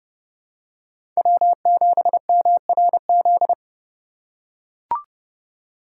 Morse code, heard as a keyed tone, about 28 wpm, 700 Hz. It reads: W7MRZ